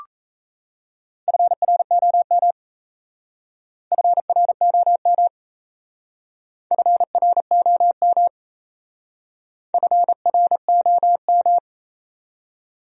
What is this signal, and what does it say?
Morse code, heard as a keyed tone, about 38 wpm, 700 Hz. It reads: FROM FROM FROM FROM